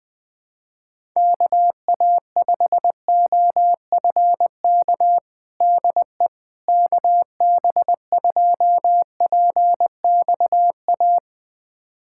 Morse code, heard as a keyed tone, about 20 wpm, 700 Hz. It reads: KA5OFK DE KB2PXA
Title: USA Callsign DE Callsign - 6 Characters - 20wpm